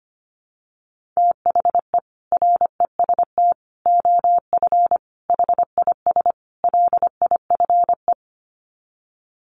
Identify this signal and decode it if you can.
Morse code, heard as a keyed tone, about 25 wpm, 700 Hz. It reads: THE REST OF HIS LIFE
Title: Sets of 5 Words 25wpm